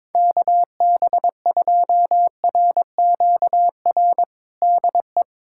Morse code, heard as a keyed tone, about 22 wpm, 700 Hz. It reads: KB2RQR DE
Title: USA Callsign DE Callsign - 6 Characters - 22wpm